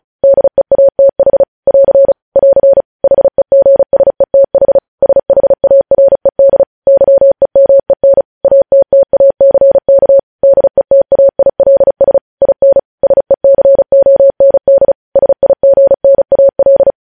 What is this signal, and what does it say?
Morse code, heard as a keyed tone, about 36 wpm, 550 Hz. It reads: DEATH <AR> <AR> HEGSETH SHARED YEMEN ATTACK DETAILS IN SECOND SIGNAL
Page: News Headlines in Morse Code at 35 WPM